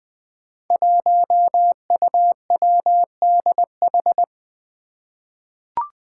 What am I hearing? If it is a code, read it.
Morse code, heard as a keyed tone, about 20 wpm, 700 Hz. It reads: 1UWDH
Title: Five Challenging Letter-Number Combinations 20wpm